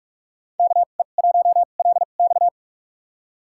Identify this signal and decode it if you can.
Morse code, heard as a keyed tone, about 46 wpm, spaced as for 33 wpm, 700 Hz. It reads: KE1LX
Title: USA Call Signs - 5 Characters - 2x Character Spacing - 45wpm